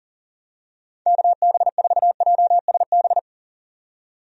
Morse code, heard as a keyed tone, about 40 wpm, 700 Hz. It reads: KB4JSB
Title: USA Call Signs - 6 Characters - 40wpm